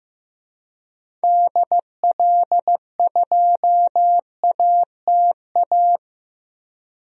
Morse code, heard as a keyed tone, about 15 wpm, 700 Hz. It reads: DL2ATA
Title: DXCC-100 Call Signs - 6 Characters - Repeated 2x - 15wpm